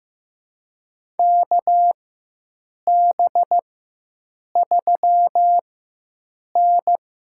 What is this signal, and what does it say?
Morse code, heard as a keyed tone, about 15 wpm, spaced as for 7 wpm, 700 Hz. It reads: KB3N
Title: USA Call Signs - 4 Characters - 4x Character Spacing - 15wpm